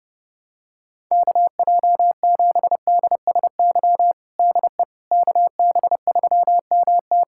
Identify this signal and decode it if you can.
Morse code, heard as a keyed tone, about 30 wpm, 700 Hz. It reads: KJ7DSY DE KB3MT
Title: K1USN SST QSO Practice 30wpm